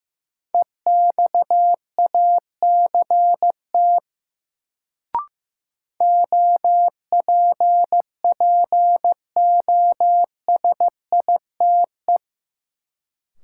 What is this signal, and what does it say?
Morse code, heard as a keyed tone, about 15 wpm, 700 Hz. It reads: EXACT OPPOSITE
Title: Sentences from Top 1300 Words - New Word Drill Alt - 15wpm